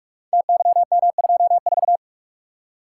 Morse code, heard as a keyed tone, about 44 wpm, 700 Hz. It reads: TYM24
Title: Code Groups - 5 Letter:Numbers - 45wpm